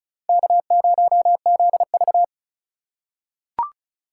Morse code, heard as a keyed tone, about 36 wpm, 700 Hz. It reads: K0ZV
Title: USA Call Signs - 4 Characters - 4x Character Spacing - 35wpm